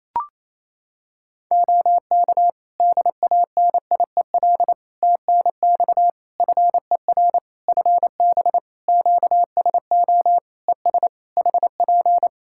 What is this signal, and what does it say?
Morse code, heard as a keyed tone, about 28 wpm, 700 Hz. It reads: OK DANIEL TNX FER FB QSO ES HP
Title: K1USN SST QSO Practice 28wpm